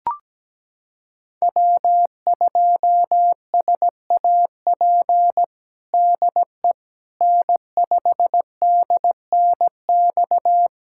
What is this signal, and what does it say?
Morse code, heard as a keyed tone, about 17 wpm, 700 Hz. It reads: W2SAP DE N5DNX
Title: USA Callsign DE Callsign - 5 Characters - 17wpm